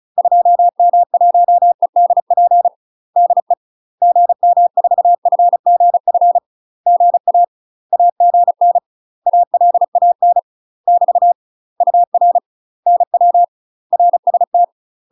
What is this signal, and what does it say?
Morse code, heard as a keyed tone, about 36 wpm, 700 Hz. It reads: 2M1EDP DE GM4FGF GA AGN ALAN <BT> UR NW RST
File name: UK MASTER QSOs TRAINING / 35WPM / UK MASTER QSOs 0028